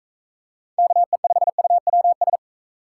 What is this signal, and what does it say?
Morse code, heard as a keyed tone, about 42 wpm, 700 Hz. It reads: KE5UWS